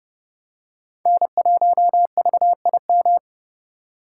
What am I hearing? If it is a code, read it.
Morse code, heard as a keyed tone, about 29 wpm, 700 Hz. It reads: N1VIM
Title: USA Call Signs - 5 Characters - 2x Character Spacing - 30wpm